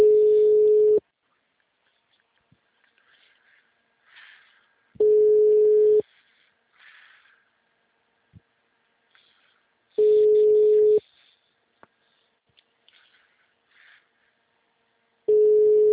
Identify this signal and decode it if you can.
Ringback tone (Europe, 425 Hz).
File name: out- -2007- - - .431